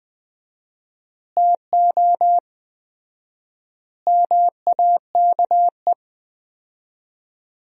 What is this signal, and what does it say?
Morse code, heard as a keyed tone, about 20 wpm, 700 Hz. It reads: TO MAKE
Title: 4x Word-Spacing - Sentences from Top 400 Words - 20wpm